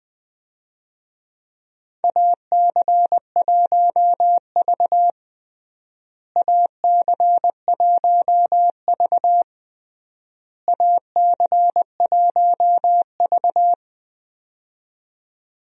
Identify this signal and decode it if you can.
Morse code, heard as a keyed tone, about 21 wpm, 700 Hz. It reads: AC1V AC1V AC1V
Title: USA Call Signs - 4 Characters - Repeated 3x - 20wpm